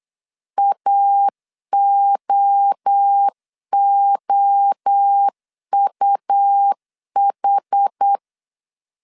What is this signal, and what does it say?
Morse code, heard as a keyed tone, about 8 wpm, 800 Hz. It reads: AOOUH
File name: CW / DET / 1 3 1500C S B 8wpm 4141